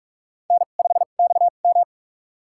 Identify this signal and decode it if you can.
Morse code, heard as a keyed tone, about 46 wpm, spaced as for 33 wpm, 700 Hz. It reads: N5XM
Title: USA Call Signs - 2x Character Spacing - 45wpm